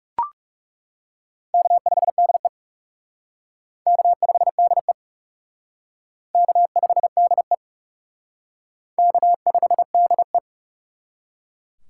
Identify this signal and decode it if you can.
Morse code, heard as a keyed tone, about 35 wpm, 700 Hz. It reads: K5DE K5DE K5DE K5DE
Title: Speed Racing - USA 1x2 Call Signs 30wpm